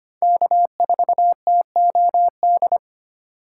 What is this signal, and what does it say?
Morse code, heard as a keyed tone, about 25 wpm, 700 Hz. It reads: K4TOD
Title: USA Call Signs - 5 Characters - 4x Character Spacing - 25wpm